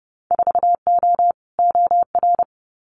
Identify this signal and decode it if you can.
Morse code, heard as a keyed tone, about 30 wpm, 700 Hz. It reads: 4O OR
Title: Winter Field Day Exchanges Encore 30wpm